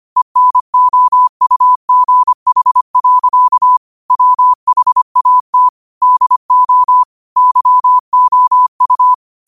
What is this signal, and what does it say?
Morse code, heard as a keyed tone, about 25 wpm, 1000 Hz. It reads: ENOUGH. WHAT DO YOU